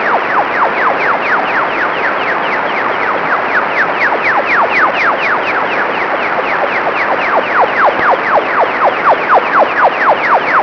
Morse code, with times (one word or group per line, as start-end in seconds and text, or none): none